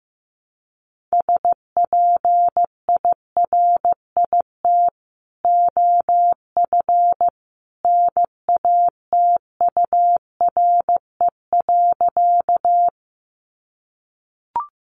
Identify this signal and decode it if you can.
Morse code, heard as a keyed tone, about 15 wpm, 700 Hz. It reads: SPIRIT OF NATURE.
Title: Sentences from Top 1400 Words 15wpm